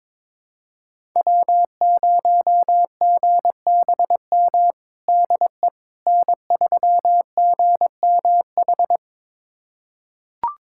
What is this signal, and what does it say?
Morse code, heard as a keyed tone, about 22 wpm, 700 Hz. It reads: W0GBM DE N3GMH
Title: USA Callsign DE Callsign - 5 Characters - 22wpm